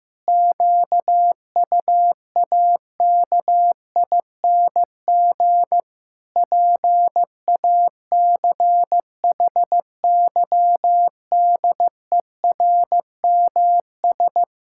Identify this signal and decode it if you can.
Morse code, heard as a keyed tone, about 15 wpm, 700 Hz. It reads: QUAKING PACHYDERMS
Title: Pangram Challenge v2 - 15wpm